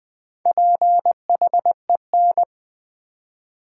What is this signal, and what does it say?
Morse code, heard as a keyed tone, about 20 wpm, 700 Hz. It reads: PHEN
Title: Build A Word - 4 Letters At A Time - 10-Letter Words - 20wpm